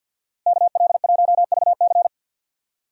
Morse code, heard as a keyed tone, about 46 wpm, 700 Hz. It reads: KL1VC
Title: USA Call Signs - 5 Characters - Repeated 2x - 50wpm